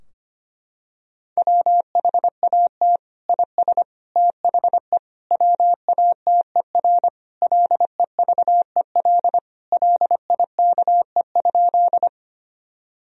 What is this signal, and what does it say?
Morse code, heard as a keyed tone, about 25 wpm, 700 Hz. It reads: WHAT IS THE WATER LEVEL LIKE?
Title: Sentences from Top 1000 Words - Review Part 4 - 25wpm